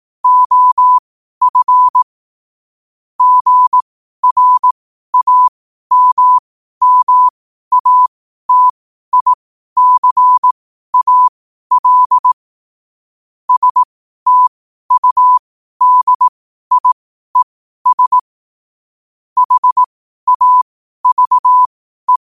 Morse code, heard as a keyed tone, about 18 wpm, spaced as for 12 wpm, 1000 Hz. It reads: OF GRAMMATICAL STUDIES HAVE
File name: qotdfiles / 12wpm